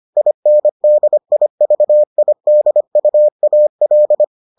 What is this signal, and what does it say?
Morse code, heard as a keyed tone, about 25 wpm, 600 Hz. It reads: INDIVIDUAL